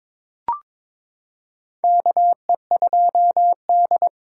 Morse code, heard as a keyed tone, about 22 wpm, 700 Hz. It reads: KE2D
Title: K1USN SST QSO Practice - 22wpm